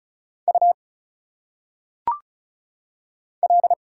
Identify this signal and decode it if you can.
Morse code, heard as a keyed tone, about 34 wpm, 700 Hz. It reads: U L
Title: Single Letters - Phonetic Pronunciation 35wpm